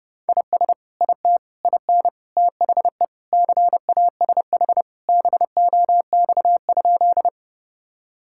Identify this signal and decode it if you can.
Morse code, heard as a keyed tone, about 32 wpm, 700 Hz. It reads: IS IT IN THE CASH BOX?
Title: Sentences from Top 1000 Words - Review Part 5 - 30wpm